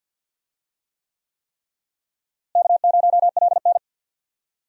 Morse code, heard as a keyed tone, about 49 wpm, 700 Hz. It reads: K0LN
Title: USA Call Signs - 4 Characters - Repeated 3x - 50wpm